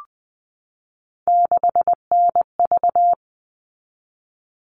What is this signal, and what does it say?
Morse code, heard as a keyed tone, about 20 wpm, 700 Hz. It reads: 6NV
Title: All Cut Numbers - 3-Digit 20wpm